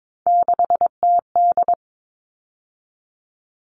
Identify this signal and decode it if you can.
Morse code, heard as a keyed tone, about 22 wpm, 700 Hz. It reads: 6TD
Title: All Cut Numbers - 3-Digit 22wpm